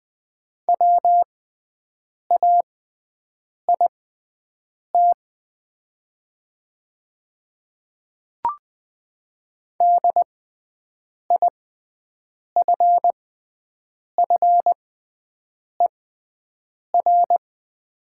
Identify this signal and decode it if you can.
Morse code, heard as a keyed tone, about 20 wpm, spaced as for 7 wpm, 700 Hz. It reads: WAIT DIFFER
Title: 6x Character Spacing - Top 500 Words - 20wpm